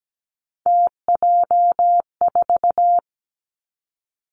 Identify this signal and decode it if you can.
Morse code, heard as a keyed tone, about 17 wpm, 700 Hz. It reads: TJ4
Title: Three Challenging Letter-Number Combinations 17wpm